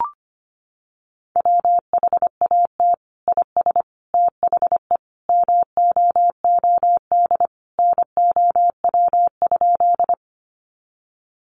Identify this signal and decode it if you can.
Morse code, heard as a keyed tone, about 25 wpm, 700 Hz. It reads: WHAT IS THE MOOD NOW?
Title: Sentences from Top 1100 Words 25wpm